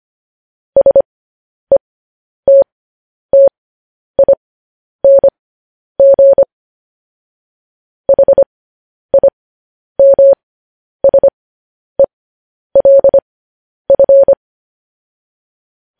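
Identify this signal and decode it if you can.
Morse code, heard as a keyed tone, about 25 wpm, spaced as for 10 wpm, 550 Hz. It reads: SETTING HIMSELF
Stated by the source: Morse news headlines sent at 10 wpm.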